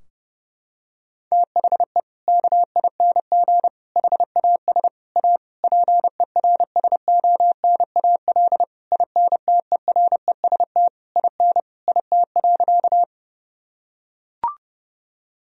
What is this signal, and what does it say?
Morse code, heard as a keyed tone, about 30 wpm, 700 Hz. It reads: THE KING HAS A PERSONAL INTEREST IN IT.
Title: Sentences from Top 1000 Words - Review Part 4 - 30wpm